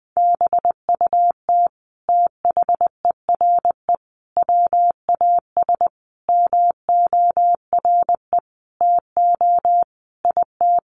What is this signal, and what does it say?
Morse code, heard as a keyed tone, about 20 wpm, 700 Hz. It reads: BUT THERE WAS MORE TO IT
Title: Sentences from Top 100 Words 20wpm